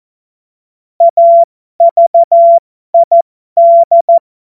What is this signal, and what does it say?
Morse code, heard as a keyed tone, about 14 wpm, spaced as for 12 wpm, 680 Hz. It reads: AVID
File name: Morsekurs / Witze